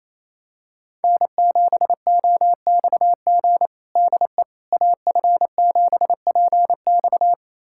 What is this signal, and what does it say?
Morse code, heard as a keyed tone, about 28 wpm, 700 Hz. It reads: N7OXG DE AF7PX
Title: USA Callsign DE Callsign - 5 Characters - 28wpm